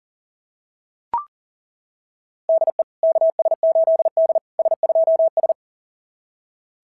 Morse code, heard as a keyed tone, about 39 wpm, 690 Hz. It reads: DE KS8D S2S
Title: SOTA QSO Practice 40wpm